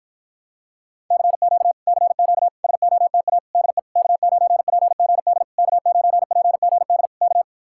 Morse code, heard as a keyed tone, about 48 wpm, 700 Hz. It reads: CQ CQ SOTA DE K9PGD K9PGD K